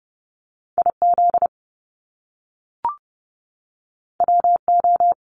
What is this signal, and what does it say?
Morse code, heard as a keyed tone, about 30 wpm, 700 Hz. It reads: I Z W O
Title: Top Two Letter Combinations in Words 30wpm